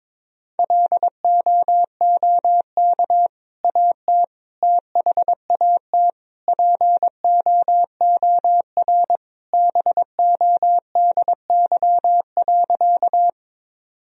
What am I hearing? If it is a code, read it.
Morse code, heard as a keyed tone, about 22 wpm, 700 Hz. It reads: LOOK AT THAT POOR BODY.
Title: Sentences from Top 1000 Words - Review Part 2 - 22wpm